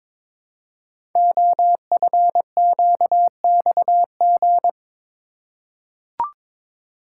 Morse code, heard as a keyed tone, about 22 wpm, 700 Hz. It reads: OFQXG
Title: Code Groups - 5 Letters - 22wpm